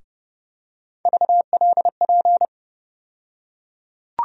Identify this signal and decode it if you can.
Morse code, heard as a keyed tone, about 32 wpm, 700 Hz. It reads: VLP
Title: USA Call Sign Postfixes - 3 Letters 30wpm